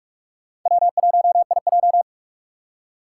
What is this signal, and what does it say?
Morse code, heard as a keyed tone, about 46 wpm, 700 Hz. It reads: W1IJ
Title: USA Call Signs - 6x Character Spacing - 45wpm